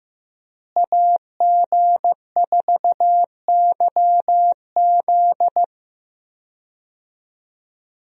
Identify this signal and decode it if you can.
Morse code, heard as a keyed tone, about 15 wpm, 700 Hz. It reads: AG4YZ
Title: USA Call Signs - 5 Characters - Repeated 2x - 15wpm